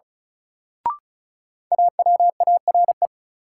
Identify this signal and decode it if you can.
Morse code, heard as a keyed tone, about 34 wpm, 700 Hz. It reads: AWARE
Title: Sentences from Top 1200 Words - New Word Drill - 35wpm